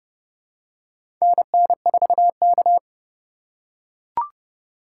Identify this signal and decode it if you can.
Morse code, heard as a keyed tone, about 32 wpm, 700 Hz. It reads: NN4K
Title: USA Call Signs - 4 Characters - 3x Character Spacing - 30wpm